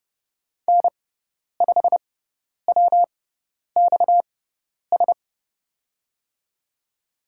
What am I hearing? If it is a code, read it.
Morse code, heard as a keyed tone, about 30 wpm, spaced as for 10 wpm, 700 Hz. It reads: N5WXS
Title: USA Call Signs - 6x Character Spacing - 30wpm